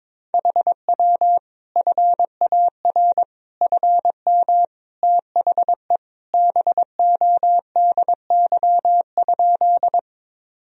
Morse code, heard as a keyed tone, about 22 wpm, 700 Hz. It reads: HW FAR FM THE BODY?
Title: Sentences from Top 400 Words and Abbreviations 22wpm